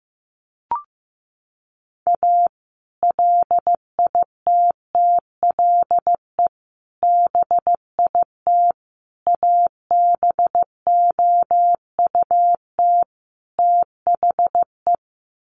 Morse code, heard as a keyed tone, about 15 wpm, 700 Hz. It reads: A LITTLE BIT ABOUT THE
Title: Sets of 5 Words Encore2 15wpm